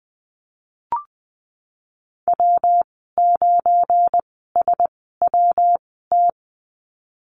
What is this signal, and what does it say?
Morse code, heard as a keyed tone, about 20 wpm, spaced as for 15 wpm, 700 Hz. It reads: W9SWT
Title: 173 Lesson 28g - Number 9 - Call Signs 20wpm